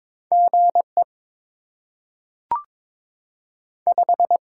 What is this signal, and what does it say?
Morse code, heard as a keyed tone, about 22 wpm, 700 Hz. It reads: G E 5
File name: Top 20 QSO Elements - 8x Character Spacing - 22wpm